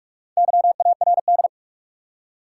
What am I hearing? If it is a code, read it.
Morse code, heard as a keyed tone, about 47 wpm, 700 Hz. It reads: YARD
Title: Four Letter Words - 45wpm